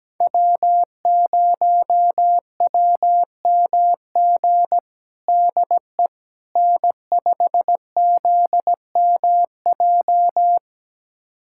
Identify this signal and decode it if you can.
Morse code, heard as a keyed tone, about 17 wpm, 700 Hz. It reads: W0WMG DE N5ZMJ
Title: USA Callsign DE Callsign - 5 Characters - 17wpm